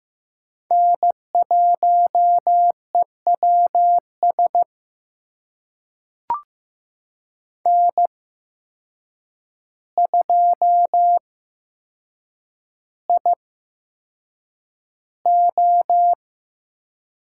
Morse code, heard as a keyed tone, about 15 wpm, spaced as for 7 wpm, 700 Hz. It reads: N1EWS N2IO
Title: USA Call Signs - 5 Characters - 8x Character Spacing - 15wpm